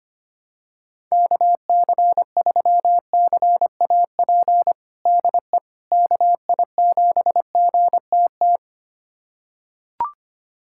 Morse code, heard as a keyed tone, about 25 wpm, 700 Hz. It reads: KC3CAP DE KI7GTT
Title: USA Callsign DE Callsign 25wpm